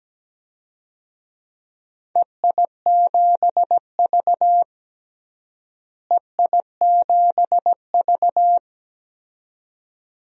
Morse code, heard as a keyed tone, about 17 wpm, 700 Hz. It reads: EI7V EI7V
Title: DXCC-100 Call Signs - 4 Characters - Repeated 2x - 17wpm